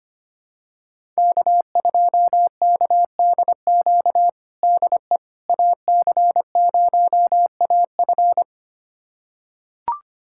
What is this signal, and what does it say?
Morse code, heard as a keyed tone, about 25 wpm, 700 Hz. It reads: K2KDQ DE AC0AF
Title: USA Callsign DE Callsign - 5 Characters - 25wpm